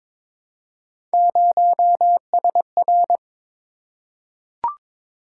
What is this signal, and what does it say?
Morse code, heard as a keyed tone, about 22 wpm, 700 Hz. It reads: 0SR
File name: practice / 22wpm / Three Letter-Number Combinations - 22wpm